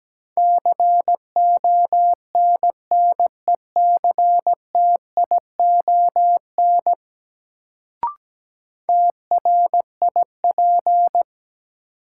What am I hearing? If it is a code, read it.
Morse code, heard as a keyed tone, about 17 wpm, 700 Hz. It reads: CONNECTION TRIP
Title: Sentences from Top 1300 Words - New Word Drill Alt - 17wpm